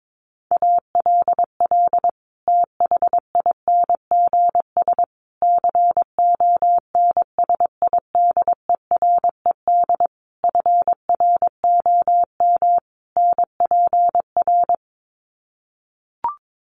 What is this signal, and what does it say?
Morse code, heard as a keyed tone, about 22 wpm, 700 Hz. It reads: ALL THINGS CONSIDERED FROM NPR
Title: Sets of 5 Words Encore 22wpm